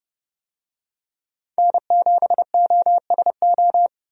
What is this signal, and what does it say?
Morse code, heard as a keyed tone, about 32 wpm, 700 Hz. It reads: N7OSO
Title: USA Call Signs - 5 Characters - 30wpm